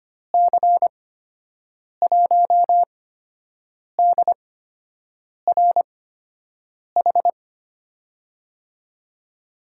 Morse code, heard as a keyed tone, about 25 wpm, spaced as for 7 wpm, 700 Hz. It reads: C1DRH
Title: USA Call Signs - 8x Character Spacing - 25wpm